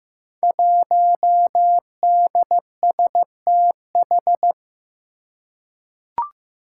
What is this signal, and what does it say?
Morse code, heard as a keyed tone, about 15 wpm, 700 Hz. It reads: 1DSTH
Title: Five Challenging Letter-Number Combinations 15wpm